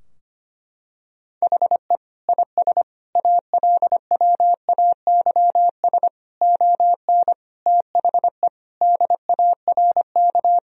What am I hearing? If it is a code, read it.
Morse code, heard as a keyed tone, about 25 wpm, 700 Hz. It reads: HE IS ALWAYS ON THE DARK
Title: Sentences from Top 1000 Words - Review Part 3 - 25wpm